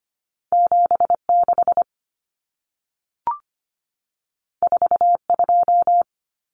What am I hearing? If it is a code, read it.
Morse code, heard as a keyed tone, about 25 wpm, 700 Hz. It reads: 76 42
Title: Numbers - 2-Digit 25wpm